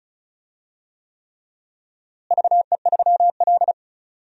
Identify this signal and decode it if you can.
Morse code, heard as a keyed tone, about 36 wpm, 700 Hz. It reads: VE3L
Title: DXCC-100 Call Signs - 4 Characters - Repeated 3x - 35wpm